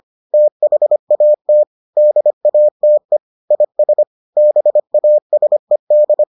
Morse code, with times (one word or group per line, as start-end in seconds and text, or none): 0.3-1.6 THAT
2.0-3.2 DATE
3.5-4.0 IS
4.4-6.2 BASED